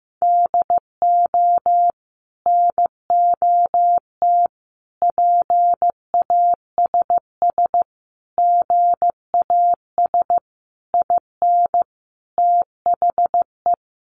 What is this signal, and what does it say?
Morse code, heard as a keyed tone, about 15 wpm, 700 Hz. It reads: DO NOT PASS GAS IN THE
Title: Sentences from Top 1100 Words 15wpm